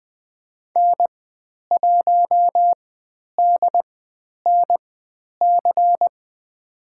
Morse code, heard as a keyed tone, about 20 wpm, spaced as for 10 wpm, 700 Hz. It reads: N1DNC
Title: 095 Lesson 16g - Letter c - Call Signs 20wpm